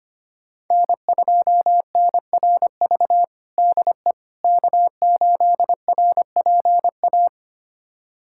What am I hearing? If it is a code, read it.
Morse code, heard as a keyed tone, about 25 wpm, 700 Hz. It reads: N2NRV DE K8RPA
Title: USA Callsign DE Callsign - 5 Characters - 25wpm